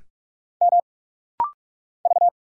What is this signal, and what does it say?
Morse code, heard as a keyed tone, about 44 wpm, 700 Hz. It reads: M V
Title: Single Letter-Number Alt - Rapid-Fire 45wpm